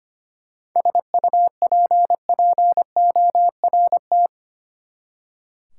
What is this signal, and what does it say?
Morse code, heard as a keyed tone, about 25 wpm, 700 Hz. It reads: SUPPORT
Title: Practice QSO - Exchange 3 - More Info - 25wpm